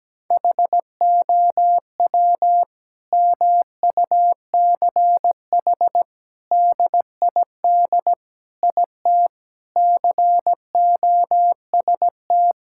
Morse code, heard as a keyed tone, about 17 wpm, 700 Hz. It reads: HOW MUCH DID IT COST
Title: Sentences from Top 1000 Words - Review Part 5 - 17wpm